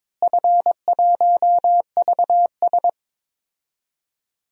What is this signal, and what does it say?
Morse code, heard as a keyed tone, about 22 wpm, 700 Hz. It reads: F1VS
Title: DXCC-100 Call Signs - 4 Characters - Repeated 2x - 22wpm